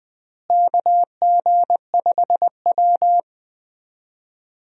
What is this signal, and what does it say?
Morse code, heard as a keyed tone, about 21 wpm, 700 Hz. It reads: KG5W
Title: USA Call Signs - 4 Characters - Repeated 2x - 20wpm